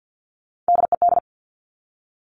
Morse code, heard as a keyed tone, about 50 wpm, 700 Hz. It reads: DED
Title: Top Three Letter Combinations in Words 50wpm